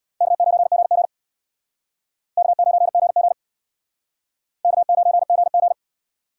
Morse code, heard as a keyed tone, about 64 wpm, 700 Hz. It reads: K9GG K9GG K9GG